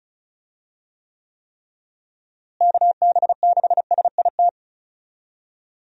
Morse code, heard as a keyed tone, about 36 wpm, 700 Hz. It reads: KB6SIT
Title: USA Call Signs - 6 Characters - Repeated 3x - 35wpm